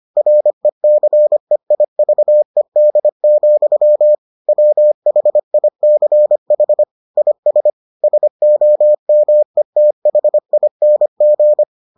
Morse code, heard as a keyed tone, about 25 wpm, 600 Hz. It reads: RECEIVED, WHICH IS SOMETHING